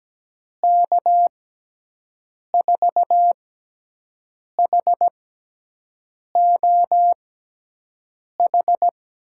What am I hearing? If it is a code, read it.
Morse code, heard as a keyed tone, about 17 wpm, 700 Hz. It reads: K 4 H O H
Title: USA Call Signs - 6 Characters - 6x Character Spacing - 17wpm